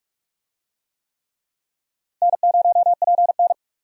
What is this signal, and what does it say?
Morse code, heard as a keyed tone, about 48 wpm, 700 Hz. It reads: N0PN
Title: USA Call Signs - 4 Characters - 45wpm